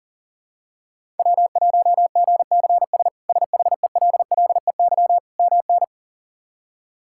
Morse code, heard as a keyed tone, about 40 wpm, 700 Hz. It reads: W1GCS SHELLEY MN
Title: K1USN SST QSO Practice - 40wpm